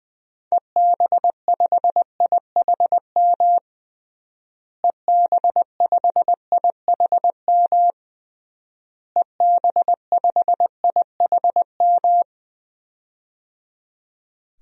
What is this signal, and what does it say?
Morse code, heard as a keyed tone, about 21 wpm, 700 Hz. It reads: EB5IHM EB5IHM EB5IHM
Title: DXCC-100 Call Signs - 6 Characters - Repeated 3x - 20wpm